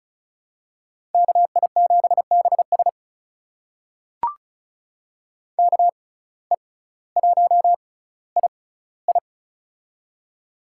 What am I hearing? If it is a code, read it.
Morse code, heard as a keyed tone, about 36 wpm, spaced as for 18 wpm, 700 Hz. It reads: KI7BS KE1II
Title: USA Call Signs - 5 Characters - 6x Character Spacing - 35wpm